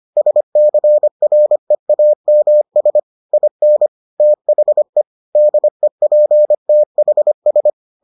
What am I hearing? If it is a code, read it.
Morse code, heard as a keyed tone, about 25 wpm, 600 Hz. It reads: SCREAMS IN THE DEPTHS